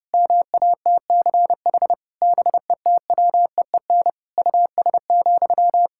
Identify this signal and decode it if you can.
Morse code, heard as a keyed tone, about 30 wpm, 700 Hz. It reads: MATCH BETWEEN US,
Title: Sentences from Top 1600 Words 30wpm